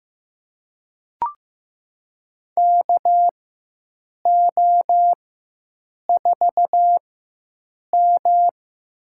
Morse code, heard as a keyed tone, about 15 wpm, spaced as for 7 wpm, 700 Hz. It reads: KO4M
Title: USA Call Signs - 4 Characters - 4x Character Spacing - 15wpm